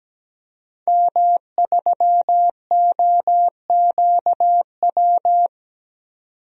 Morse code, heard as a keyed tone, about 17 wpm, 700 Hz. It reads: M3OQW
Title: DXCC-100 Call Signs - 5 Characters - 17wpm